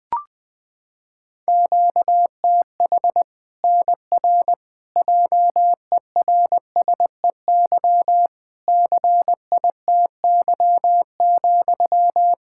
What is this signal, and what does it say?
Morse code, heard as a keyed tone, about 20 wpm, 700 Hz. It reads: QTH NR JERSEY CITY,
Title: K1USN SST QSO Practice 20wpm